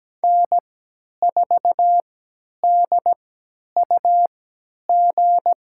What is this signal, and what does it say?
Morse code, heard as a keyed tone, about 17 wpm, spaced as for 10 wpm, 700 Hz. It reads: N4DUG
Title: USA Call Signs - 5 Characters - 3x Character Spacing - 17wpm